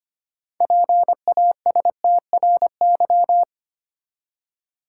Morse code, heard as a keyed tone, about 25 wpm, 690 Hz. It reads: PASTRY
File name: practice / 25wpm / Easy Words with Hints - 25wpm